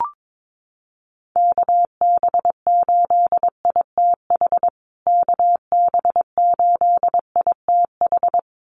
Morse code, heard as a keyed tone, about 22 wpm, 700 Hz. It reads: KB8ITH KB8ITH
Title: USA Call Signs Repeated 2x 22wpm